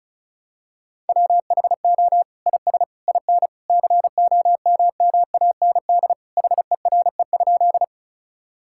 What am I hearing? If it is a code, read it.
Morse code, heard as a keyed tone, about 36 wpm, 700 Hz. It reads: WHO IS IN COMMAND HERE?
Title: Sentences from Top 1600 Words 35wpm